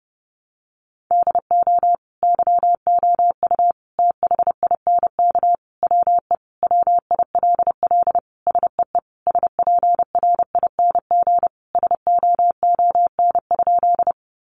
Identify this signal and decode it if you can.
Morse code, heard as a keyed tone, about 30 wpm, 700 Hz. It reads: DO YOU THINK WE WILL SEE SPRING SOON?
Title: Sentences from Top 1200 Words 30wpm